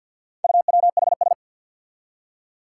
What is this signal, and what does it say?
Morse code, heard as a keyed tone, about 53 wpm, 700 Hz. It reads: UWHS
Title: Four Challenging Letter-Number Combinations 50wpm